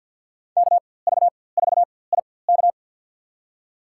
Morse code, heard as a keyed tone, about 53 wpm, spaced as for 24 wpm, 700 Hz. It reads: KV4IK
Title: USA Call Signs - 5 Characters - 4x Character Spacing - 50wpm